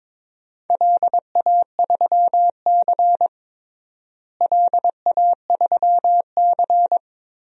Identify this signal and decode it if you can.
Morse code, heard as a keyed tone, about 22 wpm, 700 Hz. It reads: LA3C LA3C
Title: DXCC-100 Call Signs - 4 Characters - Repeated 2x - 22wpm